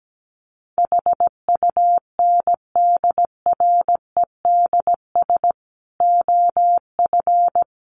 Morse code, heard as a keyed tone, about 17 wpm, 700 Hz. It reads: HUNDREDS OF
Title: Sets of 5 Words Encore 17wpm